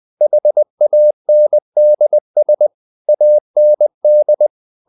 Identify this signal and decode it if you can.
Morse code, heard as a keyed tone, about 20 wpm, 600 Hz. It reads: HANDS AND
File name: the old man and the sea part 2 20wpm